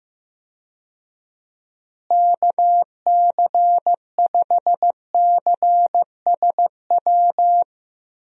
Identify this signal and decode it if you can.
Morse code, heard as a keyed tone, about 16 wpm, 700 Hz. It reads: KC5CSW
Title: USA Call Signs - 6 Characters - Repeated 3x - 15wpm